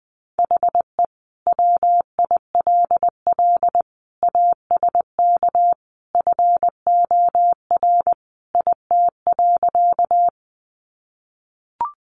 Morse code, heard as a keyed tone, about 19 wpm, 700 Hz. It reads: HE WILL ASK FOR IT.
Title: Sentences from Top 200 Words 20wpm